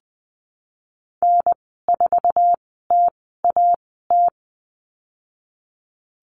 Morse code, heard as a keyed tone, about 20 wpm, spaced as for 14 wpm, 700 Hz. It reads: N4TAT
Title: 048 Lesson 9g - Number 4 - Call Signs 20wpm